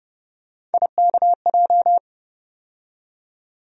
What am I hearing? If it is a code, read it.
Morse code, heard as a keyed tone, about 30 wpm, 700 Hz. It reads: IKJ
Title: USA Call Sign Postfixes - 3 Letters 30wpm